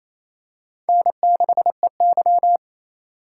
Morse code, heard as a keyed tone, about 28 wpm, 700 Hz. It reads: N6EY
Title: USA Call Signs - 4x Character Spacing - 28wpm